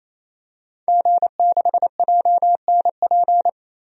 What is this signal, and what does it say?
Morse code, heard as a keyed tone, about 28 wpm, 700 Hz. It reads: G6JNP
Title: DXCC-100 Call Signs - 5 Characters - 28wpm